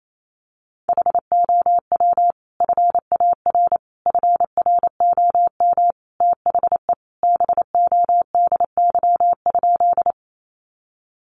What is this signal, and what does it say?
Morse code, heard as a keyed tone, about 28 wpm, 700 Hz. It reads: HOW FAR FROM THE BODY?
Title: Sentences from Top 400 Words 28wpm